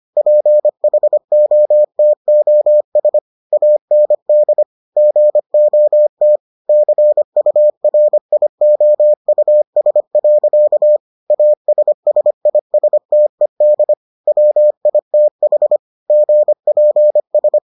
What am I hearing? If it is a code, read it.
Morse code, heard as a keyed tone, about 25 wpm, 600 Hz. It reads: PHOTOS AND GOT CURIOUS. ASSISTED WITH GPS